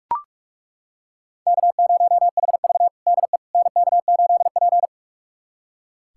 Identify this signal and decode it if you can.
Morse code, heard as a keyed tone, about 41 wpm, 700 Hz. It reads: K0HV DE NK8P
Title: USA Callsign DE Callsign - 4 Characters - 45wpm